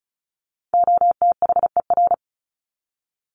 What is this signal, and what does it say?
Morse code, heard as a keyed tone, about 34 wpm, 700 Hz. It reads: OTHER
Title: Top 500 Words 35wpm